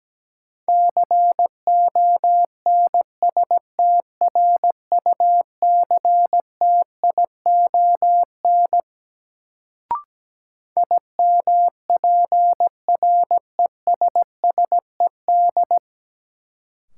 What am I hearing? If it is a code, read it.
Morse code, heard as a keyed tone, about 17 wpm, 700 Hz. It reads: CONSTRUCTION IMPRESSED
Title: Sentences from Top 1600 Words - New Word Drill - 17wpm